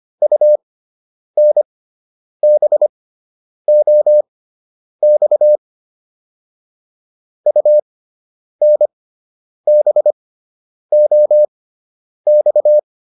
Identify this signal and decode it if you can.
Morse code, heard as a keyed tone, about 25 wpm, spaced as for 9 wpm, 600 Hz. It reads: UNBOX UNBOX